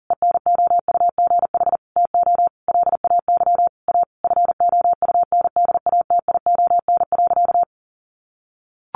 Morse code, heard as a keyed tone, about 40 wpm, 700 Hz. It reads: ENOUGH TO LAY A FOUNDATION.
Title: Sentences from Top 1200 Words 40wpm